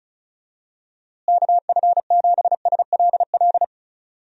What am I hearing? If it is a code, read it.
Morse code, heard as a keyed tone, about 36 wpm, 700 Hz. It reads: KF7SLL
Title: USA Call Signs - 6 Characters - Repeated 2x - 35wpm